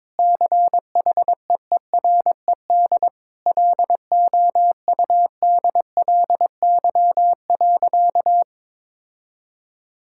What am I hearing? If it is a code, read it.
Morse code, heard as a keyed tone, about 22 wpm, 700 Hz. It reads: CHEERED LOUDLY.
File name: practice / 22wpm / Sentences with 2-Word Phrases - 22wpm